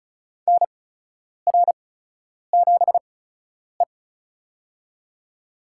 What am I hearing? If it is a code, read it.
Morse code, heard as a keyed tone, about 36 wpm, spaced as for 10 wpm, 700 Hz. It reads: NR7E
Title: USA Call Signs - 8x Character Spacing - 35wpm